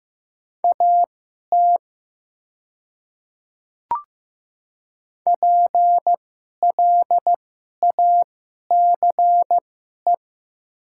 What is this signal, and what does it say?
Morse code, heard as a keyed tone, about 15 wpm, spaced as for 11 wpm, 700 Hz. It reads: AT PLACE
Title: 2x Character Spacing - Top 200 Words - 15wpm